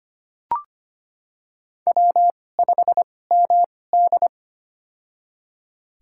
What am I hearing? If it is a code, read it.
Morse code, heard as a keyed tone, about 25 wpm, spaced as for 18 wpm, 700 Hz. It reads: W5MD
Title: USA Call Signs - 4 Characters - 2x Character Spacing - 25wpm